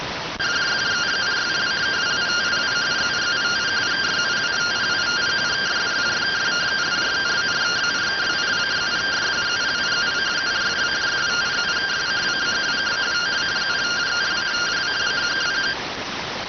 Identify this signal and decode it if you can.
Two-tone teleprinter signal, about 170 Hz shift.